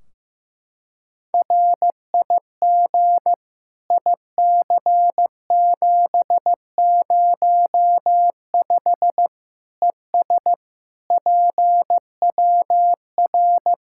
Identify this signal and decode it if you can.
Morse code, heard as a keyed tone, about 15 wpm, 700 Hz. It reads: RIG IC705 ES PWR
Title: K1USN SST QSO Practice 15wpm